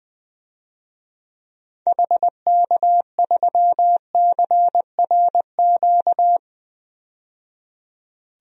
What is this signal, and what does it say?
Morse code, heard as a keyed tone, about 20 wpm, 700 Hz. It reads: HK3CRQ
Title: DXCC-100 Call Signs - 6 Characters - 20wpm